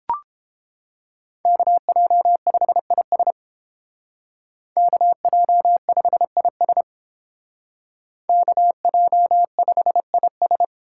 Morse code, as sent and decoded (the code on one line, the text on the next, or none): -.- .--- ..... .. ... / -.- .--- ..... .. ... / -.- .--- ..... .. ...
KJ5IS KJ5IS KJ5IS